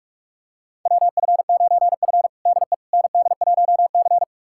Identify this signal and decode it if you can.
Morse code, heard as a keyed tone, about 46 wpm, 700 Hz. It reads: WF9F DE ND1C
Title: USA Callsign DE Callsign - 4 Characters - 45wpm